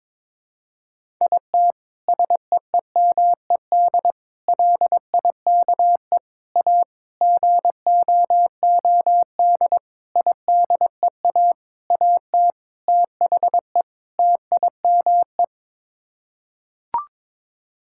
Morse code, heard as a keyed tone, about 22 wpm, 700 Hz. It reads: IT SEEMED LIKE A GOOD IDEA AT THE TIME
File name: Sets of 9 Words - 22wpm